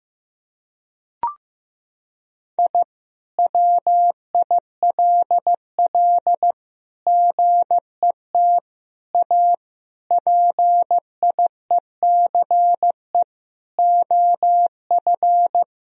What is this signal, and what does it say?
Morse code, heard as a keyed tone, about 15 wpm, 700 Hz. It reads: I WILL GET A PIECE OF
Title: Sentences from Top 1000 Words - Review Part 2 - 15wpm